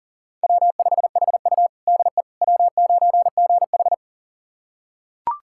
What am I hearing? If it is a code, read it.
Morse code, heard as a keyed tone, about 40 wpm, 700 Hz. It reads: W5HU DE W9GH
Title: USA Callsign DE Callsign - 4 Characters - 40wpm